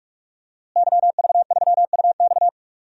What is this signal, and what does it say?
Morse code, heard as a keyed tone, about 46 wpm, 700 Hz. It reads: YV3UX
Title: DXCC-100 Call Signs - 5 Characters - Repeated 3x - 45wpm